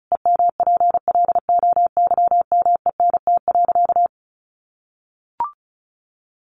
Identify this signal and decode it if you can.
Morse code, heard as a keyed tone, about 34 wpm, 700 Hz. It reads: EMPLOYMENT.
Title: Sentences from Top 1500 Words 35wpm